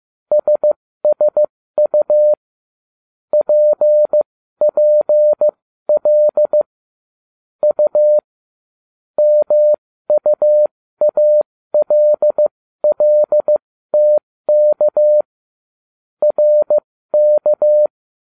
Morse code, heard as a keyed tone, about 15 wpm, spaced as for 13 wpm, 600 Hz. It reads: SSU PPL U MUALLTK RK